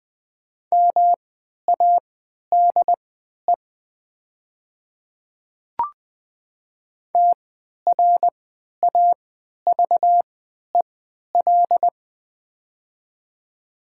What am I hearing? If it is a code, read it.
Morse code, heard as a keyed tone, about 20 wpm, spaced as for 11 wpm, 700 Hz. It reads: MADE TRAVEL
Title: 3x Character Spacing - Top 500 Words - 20wpm